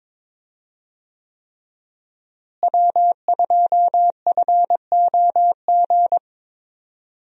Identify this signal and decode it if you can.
Morse code, heard as a keyed tone, about 21 wpm, 700 Hz. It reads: W2FOG
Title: USA Call Signs - 5 Characters - Repeated 3x - 22wpm